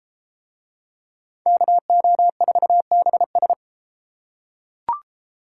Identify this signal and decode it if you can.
Morse code, heard as a keyed tone, about 32 wpm, 700 Hz. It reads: KO4BS